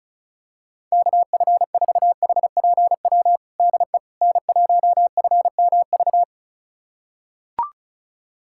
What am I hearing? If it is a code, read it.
Morse code, heard as a keyed tone, about 36 wpm, 700 Hz. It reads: KF4HPW DE N1FMV
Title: USA Callsign DE Callsign 35wpm